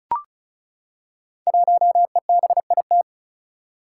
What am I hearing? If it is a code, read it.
Morse code, heard as a keyed tone, about 36 wpm, 700 Hz. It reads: 1EBIT
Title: Five Challenging Letter-Number Combinations 35wpm